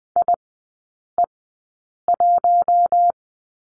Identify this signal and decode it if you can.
Morse code, heard as a keyed tone, about 20 wpm, spaced as for 8 wpm, 700 Hz. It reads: IE1
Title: 053 Lesson 10e - Letter r - Three Characters 20wpm